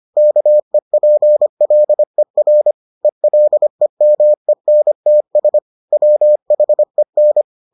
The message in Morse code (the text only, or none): KEPLER ELEMENTS WHEN